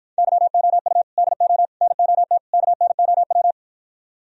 Morse code, heard as a keyed tone, about 52 wpm, 700 Hz. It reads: YOU DO NOT KNOW